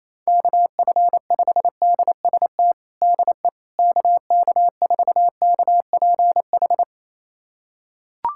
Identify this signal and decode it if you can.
Morse code, heard as a keyed tone, about 27 wpm, 700 Hz. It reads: KF5DST DE KK4KPH
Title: USA Callsign DE Callsign 28wpm